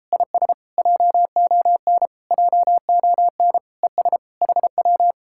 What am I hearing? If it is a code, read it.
Morse code, heard as a keyed tone, about 33 wpm, 700 Hz. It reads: IS JON JON ES HW